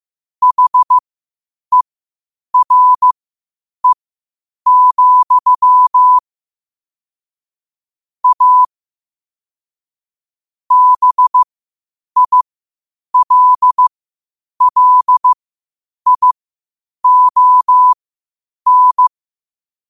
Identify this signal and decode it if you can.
Morse code, heard as a keyed tone, about 15 wpm, spaced as for 8 wpm, 1000 Hz. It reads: HERE, A BILLION
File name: qotdfiles / 8wpm